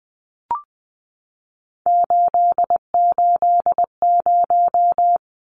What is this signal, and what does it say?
Morse code, heard as a keyed tone, about 20 wpm, 700 Hz. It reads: 880
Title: Numbers - 3-Digit 20wpm